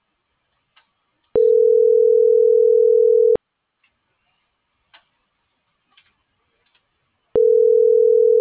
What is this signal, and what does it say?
Ringback tone (North America), 440 + 480 Hz.